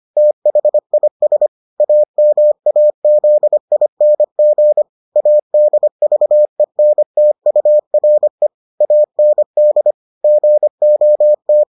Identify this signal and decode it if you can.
Morse code, heard as a keyed tone, about 25 wpm, 600 Hz. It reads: THIS AMAZING ADVENTURE AND GOT